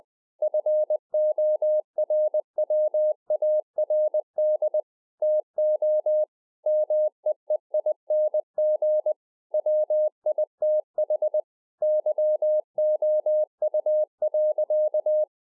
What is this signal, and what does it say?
Morse code, heard as a keyed tone, about 20 wpm, 610 Hz. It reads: FORWARD TO MEEING WITH YOU.